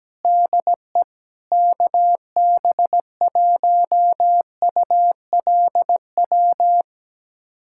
Morse code, heard as a keyed tone, about 17 wpm, 700 Hz. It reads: DE KB1ULW
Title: USA Callsign DE Callsign 17wpm